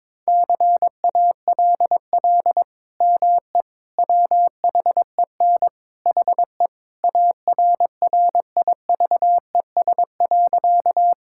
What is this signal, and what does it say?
Morse code, heard as a keyed tone, about 21 wpm, 700 Hz. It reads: CALL ME WHEN HE ARRIVES.
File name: practice / 22wpm / Sentences with 2-Word Phrases - 22wpm